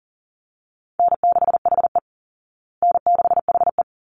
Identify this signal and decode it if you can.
Morse code, heard as a keyed tone, about 40 wpm, 700 Hz. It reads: N6HE N6HE
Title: Call Signs Repeated 3x - 40wpm